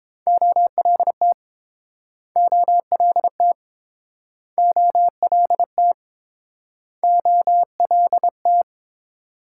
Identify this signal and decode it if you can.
Morse code, heard as a keyed tone, about 30 wpm, 700 Hz. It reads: OLT OLT OLT OLT